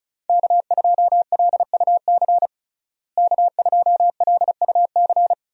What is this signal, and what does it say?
Morse code, heard as a keyed tone, about 34 wpm, 700 Hz. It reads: K2LUC K2LUC
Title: USA Call Signs - 5 Characters - Repeated 3x - 35wpm